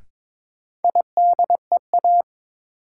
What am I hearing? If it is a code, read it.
Morse code, heard as a keyed tone, about 22 wpm, 700 Hz. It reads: IDEA
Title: Sets of 3 Words - New Word Drill - 22wpm